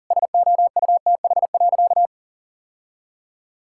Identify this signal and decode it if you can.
Morse code, heard as a keyed tone, about 37 wpm, 700 Hz. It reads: SOUTH.
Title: Sentences from Top 1000 Words - Review Part 6 - 40wpm